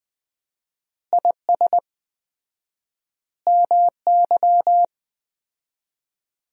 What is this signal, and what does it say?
Morse code, heard as a keyed tone, about 21 wpm, 700 Hz. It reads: IS MY
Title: 4x Word-Spacing - Sentences from Top 300 Words - 20wpm